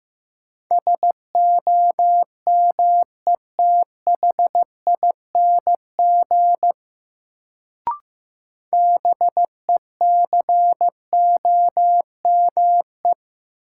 Morse code, heard as a keyed tone, about 15 wpm, 700 Hz. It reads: SOMETHING BECOME
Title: Sentences from Top 600 Words - New Word Drill - 15wpm